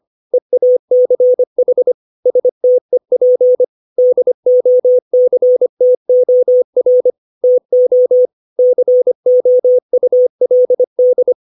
Morse code, heard as a keyed tone, about 25 wpm, 500 Hz. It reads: EACH STEP DOCTOR TO COULD
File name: 632 words / 632 Words 25WPM